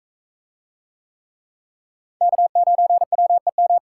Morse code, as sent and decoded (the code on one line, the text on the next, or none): -.- ----. .-- . --
K9WEM